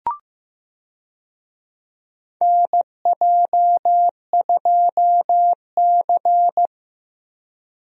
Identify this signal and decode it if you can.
Morse code, heard as a keyed tone, about 15 wpm, 700 Hz. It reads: NJ2C
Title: USA Call Signs - 4 Characters - Repeated 3x - 15wpm